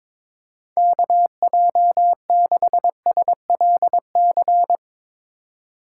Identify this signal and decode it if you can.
Morse code, heard as a keyed tone, about 22 wpm, 700 Hz. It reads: KJ6SLC
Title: Speed Racing - USA 2x3 Call Signs 22wpm